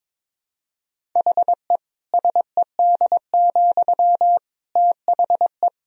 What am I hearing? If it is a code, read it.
Morse code, heard as a keyed tone, about 21 wpm, 700 Hz. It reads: HE SED, THE
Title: Sentences from Top 400 Words and Abbreviations 22wpm